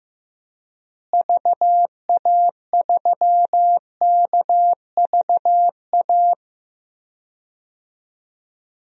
Morse code, heard as a keyed tone, about 15 wpm, 700 Hz. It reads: VA3KVA
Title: DXCC-100 Call Signs - 6 Characters - Repeated 2x - 15wpm